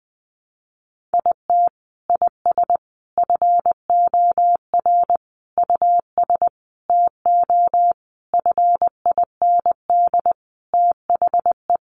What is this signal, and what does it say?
Morse code, heard as a keyed tone, about 20 wpm, 700 Hz. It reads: IT IS FOR US TO FIND THE
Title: Sentences from Top 1300 Words 20wpm